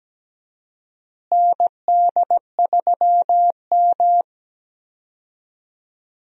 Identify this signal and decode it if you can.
Morse code, heard as a keyed tone, about 17 wpm, 700 Hz. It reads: ND3M
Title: USA Call Signs - 4 Characters - Repeated 3x - 17wpm